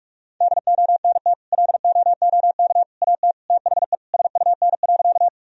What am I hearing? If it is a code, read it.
Morse code, heard as a keyed tone, about 43 wpm, 700 Hz. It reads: DONT LOOK AT THE SUN.
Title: Sentences from Top 1000 Words - Review Part 1 - 45wpm